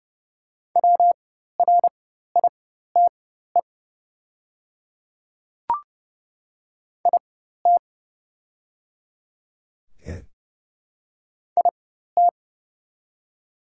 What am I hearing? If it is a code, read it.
Morse code, heard as a keyed tone, about 30 wpm, spaced as for 14 wpm, 700 Hz. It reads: WRITE IT IT
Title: 4x Character Spacing - Top 200 Words - 30wpm